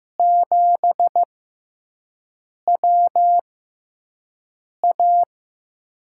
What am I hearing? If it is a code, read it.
Morse code, heard as a keyed tone, about 15 wpm, spaced as for 5 wpm, 700 Hz. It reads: 7WA
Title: USA Call Signs - 5 Characters - 6x Character Spacing - 15wpm